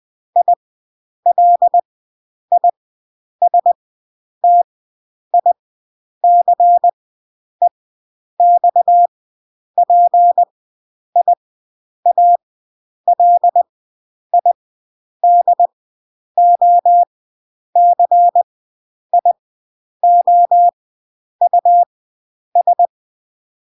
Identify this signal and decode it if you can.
Morse code, heard as a keyed tone, about 20 wpm, spaced as for 9 wpm, 700 Hz. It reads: ILISTICEXPIALIDOCIOUS